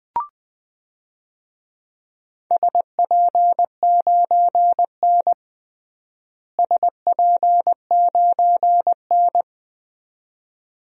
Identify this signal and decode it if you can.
Morse code, heard as a keyed tone, about 20 wpm, 700 Hz. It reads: SP9N SP9N
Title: DXCC-100 Call Signs - 4 Characters - Repeated 2x - 20wpm